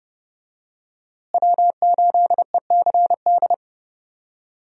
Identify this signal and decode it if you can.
Morse code, heard as a keyed tone, about 31 wpm, 700 Hz. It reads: W8ECD
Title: USA Call Signs - 5 Characters - Repeated 3x - 30wpm